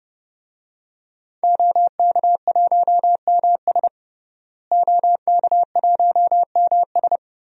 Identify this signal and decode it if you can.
Morse code, heard as a keyed tone, about 30 wpm, 700 Hz. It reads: OK1MS OK1MS
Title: DXCC-100 Call Signs - 5 Characters - Repeated 2x - 30wpm